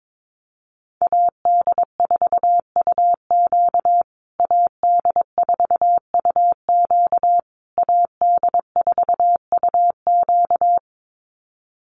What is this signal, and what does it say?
Morse code, heard as a keyed tone, about 22 wpm, 700 Hz. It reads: AD4UQ AD4UQ AD4UQ
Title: USA Call Signs Repeated 3x 22wpm